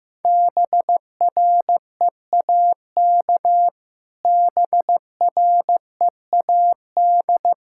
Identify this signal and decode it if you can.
Morse code, heard as a keyed tone, about 16 wpm, 700 Hz. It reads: BREAK BREAD
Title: Sentences from Top 1600 Words 15wpm